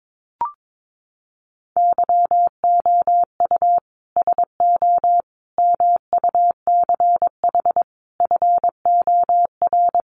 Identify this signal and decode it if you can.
Morse code, heard as a keyed tone, about 22 wpm, 700 Hz. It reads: YOU SO MUCH FOR
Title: Sets of 5 Words Encore2 22wpm